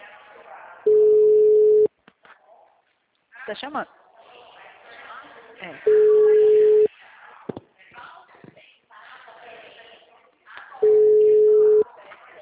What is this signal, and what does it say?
Ringback tone (Europe, 425 Hz), with speech.
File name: out- -2041- - - .30912